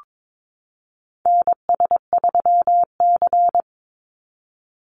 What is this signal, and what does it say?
Morse code, heard as a keyed tone, about 22 wpm, 700 Hz. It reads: NS3C
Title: Call Signs Repeated 2x - 22wpm